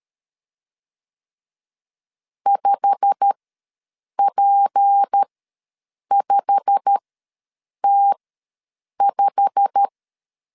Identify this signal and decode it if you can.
Morse code, heard as a keyed tone, about 13 wpm, spaced as for 7 wpm, 800 Hz. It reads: 5P5T5
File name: LEZIONI / 8 YJ45 / 8 YJ45 ws 12wpm 3146